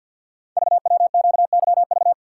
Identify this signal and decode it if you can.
Morse code, heard as a keyed tone, about 48 wpm, 690 Hz. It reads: VWQYV
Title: Code Groups - 5 Letters - 50wpm